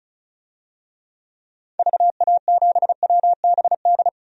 Morse code, heard as a keyed tone, about 34 wpm, 700 Hz. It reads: VA7WBD
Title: DXCC-100 Call Signs - 6 Characters - 35wpm